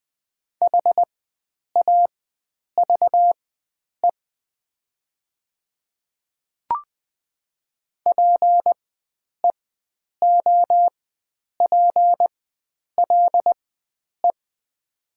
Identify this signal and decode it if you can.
Morse code, heard as a keyed tone, about 20 wpm, spaced as for 9 wpm, 700 Hz. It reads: HAVE PEOPLE
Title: 4x Character Spacing - Top 100 Words - 20wpm